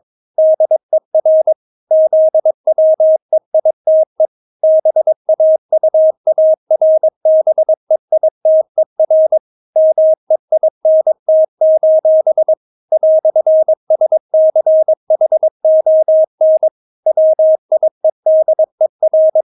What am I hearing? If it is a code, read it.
Morse code, heard as a keyed tone, about 22 wpm, 630 Hz. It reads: DER ZWEITE BAUARBEITER MEINT: "SCHON WIEDER
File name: Witz3 22wpm